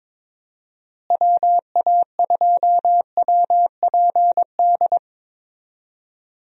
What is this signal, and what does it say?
Morse code, heard as a keyed tone, about 22 wpm, 700 Hz. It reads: WA2WPD
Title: USA Call Signs - 6 Characters - Repeated 2x - 22wpm